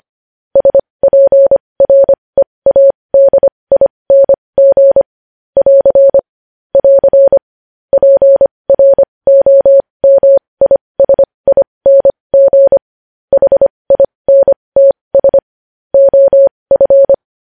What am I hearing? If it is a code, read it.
Morse code, heard as a keyed tone, about 25 wpm, spaced as for 20 wpm, 550 Hz. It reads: SPREADING <AR> <AR> PROMISING HINTS OF
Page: news-headlines-in-morse-code-at-20-wpm